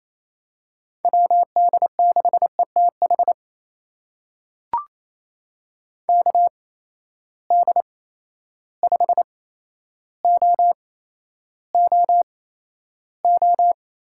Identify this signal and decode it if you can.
Morse code, heard as a keyed tone, about 28 wpm, spaced as for 12 wpm, 700 Hz. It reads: WD6ETH KD5OOO
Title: USA Call Signs - 6 Characters - 8x Character Spacing - 28wpm